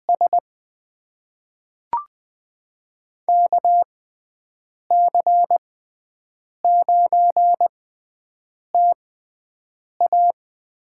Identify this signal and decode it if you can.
Morse code, heard as a keyed tone, about 20 wpm, spaced as for 7 wpm, 700 Hz. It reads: S KC9TA
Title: USA Call Signs - 6 Characters - 6x Character Spacing - 20wpm